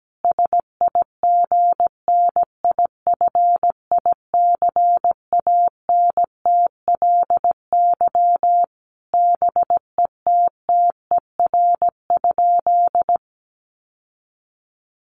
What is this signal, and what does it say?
Morse code, heard as a keyed tone, about 17 wpm, 700 Hz. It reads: SIGNIFICANTLY BETTER?
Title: Sentences from Top 1500 Words 17wpm